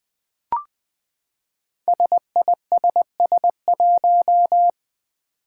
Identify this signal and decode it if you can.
Morse code, heard as a keyed tone, about 20 wpm, 700 Hz. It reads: SISS1
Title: Five Challenging Letter-Number Combinations 20wpm